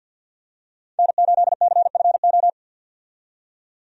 Morse code, heard as a keyed tone, about 52 wpm, 700 Hz. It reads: N8CFO
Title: USA Call Signs - 5 Characters - 50wpm